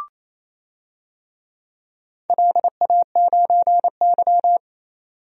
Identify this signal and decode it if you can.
Morse code, heard as a keyed tone, about 28 wpm, 700 Hz. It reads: LA9Y
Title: DXCC-100 Call Signs - 4 Characters - 28wpm